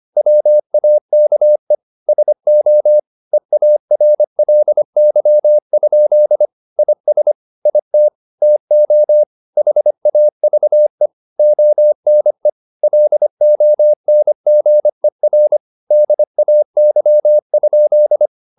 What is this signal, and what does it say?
Morse code, heard as a keyed tone, about 25 wpm, 600 Hz. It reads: WAKE SO EARLY? IS IT TO HAVE ONE LONGER DAY?
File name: the old man and the sea part 1 25wpm